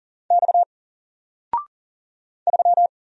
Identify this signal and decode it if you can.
Morse code, heard as a keyed tone, about 41 wpm, 700 Hz. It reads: X 3
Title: Single Letter-Number Alt 40wpm